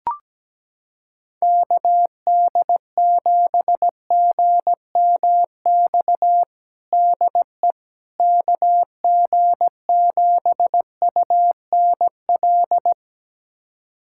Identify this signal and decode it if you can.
Morse code, heard as a keyed tone, about 17 wpm, 700 Hz. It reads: KD7GMX DE KG7UNL
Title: USA Callsign DE Callsign - 6 Characters - 17wpm